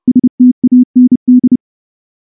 Morse code, heard as a keyed tone, about 29 wpm, 250 Hz. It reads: STAND
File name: morse / morse-training-30WPM